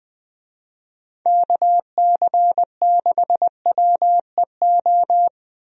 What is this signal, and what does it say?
Morse code, heard as a keyed tone, about 21 wpm, 700 Hz. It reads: KC6WEO
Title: USA Call Signs - 6 Characters - Repeated 2x - 20wpm